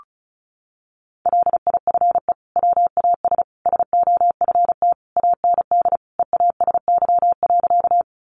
Morse code, heard as a keyed tone, about 34 wpm, 700 Hz. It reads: LIFE WAS SOFT AND EASY.
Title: Sentences from Top 1500 Words 35wpm